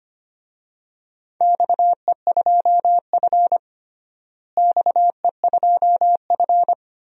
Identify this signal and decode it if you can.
Morse code, heard as a keyed tone, about 25 wpm, 700 Hz. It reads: XE2F XE2F
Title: DXCC-100 Call Signs - 4 Characters - Repeated 3x - 25wpm